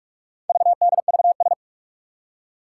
Morse code, heard as a keyed tone, about 44 wpm, 700 Hz. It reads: VDVS
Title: Four Challenging Letter-Number Combinations 45wpm